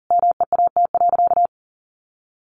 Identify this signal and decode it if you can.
Morse code, heard as a keyed tone, about 40 wpm, 700 Hz. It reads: MEAT.
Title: Sentences from Top 1200 Words 40wpm